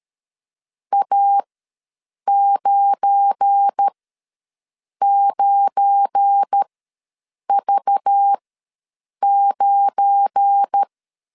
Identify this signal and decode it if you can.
Morse code, heard as a keyed tone, about 13 wpm, spaced as for 7 wpm, 800 Hz. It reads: A99V9
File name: LEZIONI / 7 GF9Z / 7 GF9 ws 12wpm 3344